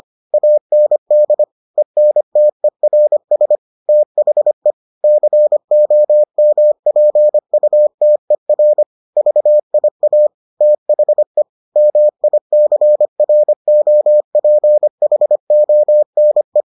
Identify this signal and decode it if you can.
Morse code, heard as a keyed tone, about 25 wpm, 600 Hz. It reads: AND ENTERS THE COMPUTER VIA THE MICROPHONE